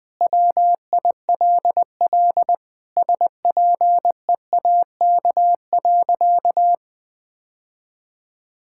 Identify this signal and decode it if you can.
Morse code, heard as a keyed tone, about 20 wpm, 700 Hz. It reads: WILL SPEAK.
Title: Sentences with 2-Word Phrases - Spoken Beginning - 20wpm